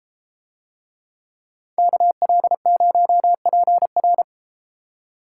Morse code, heard as a keyed tone, about 33 wpm, 700 Hz. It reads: KL0PR